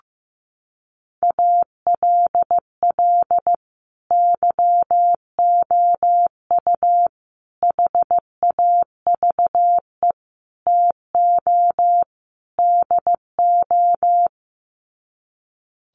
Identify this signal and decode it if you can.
Morse code, heard as a keyed tone, about 15 wpm, 700 Hz. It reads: ALL YOU HAVE TO DO
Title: Sets of 5 Words 15wpm